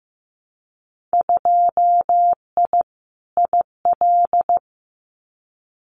Morse code, heard as a keyed tone, about 15 wpm, 700 Hz. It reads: 2I IL
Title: Winter Field Day Exchanges 15wpm